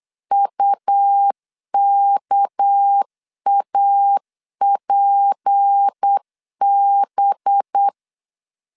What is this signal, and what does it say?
Morse code, heard as a keyed tone, about 8 wpm, 800 Hz. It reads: UKAPB
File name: CW / DET / 1 6 1500C S B 8wpm 5142